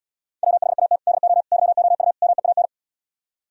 Morse code, heard as a keyed tone, about 78 wpm, 690 Hz. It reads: WHAT DO YOU MEAN